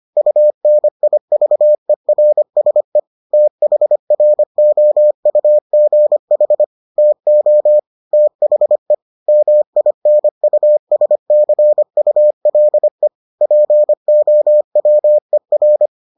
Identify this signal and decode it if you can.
Morse code, heard as a keyed tone, about 25 wpm, 600 Hz. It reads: UNIVERSE THROUGH TO THE MINUSCULE POWER